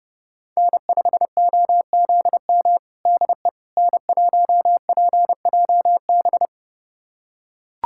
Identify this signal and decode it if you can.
Morse code, heard as a keyed tone, about 30 wpm, 700 Hz. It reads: N5OZM DE N1PJB
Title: USA Callsign DE Callsign - 5 Characters - 30wpm